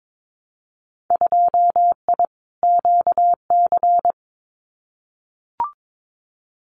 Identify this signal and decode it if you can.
Morse code, heard as a keyed tone, about 22 wpm, 700 Hz. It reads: 2I QC
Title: Winter Field Day Exchanges 22wpm